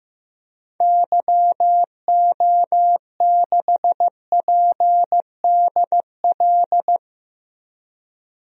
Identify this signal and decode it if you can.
Morse code, heard as a keyed tone, about 15 wpm, 700 Hz. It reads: YO6PDL
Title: DXCC-100 Call Signs - 6 Characters - Repeated 2x - 15wpm